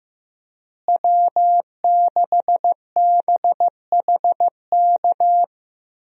Morse code, heard as a keyed tone, about 15 wpm, 700 Hz. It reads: W6BHK
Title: USA Call Signs - 5 Characters - Repeated 2x - 15wpm